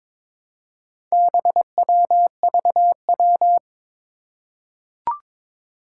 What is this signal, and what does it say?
Morse code, heard as a keyed tone, about 22 wpm, 700 Hz. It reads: BWVW
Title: Four Challenging Letter-Number Combinations 22wpm